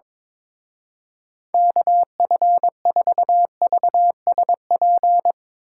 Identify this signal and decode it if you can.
Morse code, heard as a keyed tone, about 22 wpm, 700 Hz. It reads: KF4VSP
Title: Speed Racing - USA 2x3 Call Signs 22wpm